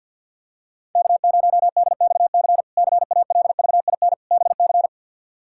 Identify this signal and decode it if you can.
Morse code, heard as a keyed tone, about 49 wpm, 700 Hz. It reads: K0DXC CALVIN BC
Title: K1USN SST QSO Practice - 50wpm